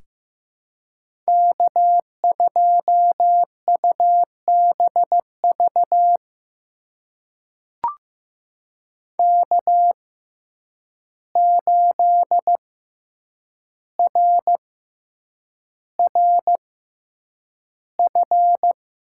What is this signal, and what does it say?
Morse code, heard as a keyed tone, about 15 wpm, spaced as for 8 wpm, 700 Hz. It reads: K2UBV K8RRF
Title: USA Call Signs - 5 Characters - 6x Character Spacing - 15wpm